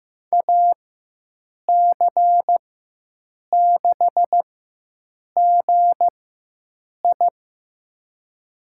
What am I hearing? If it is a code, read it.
Morse code, heard as a keyed tone, about 15 wpm, spaced as for 7 wpm, 700 Hz. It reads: AC6GI
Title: USA Call Signs - 5 Characters - 4x Character Spacing - 15wpm